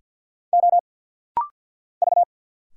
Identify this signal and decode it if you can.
Morse code, heard as a keyed tone, about 50 wpm, 700 Hz. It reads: O V
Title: Single Letters - Rapid-Fire 50wpm